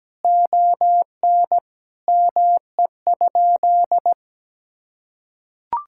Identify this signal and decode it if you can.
Morse code, heard as a keyed tone, about 17 wpm, 700 Hz. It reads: ON ME?
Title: Sentences from Top 1000 Words - Review Part 5 - 17wpm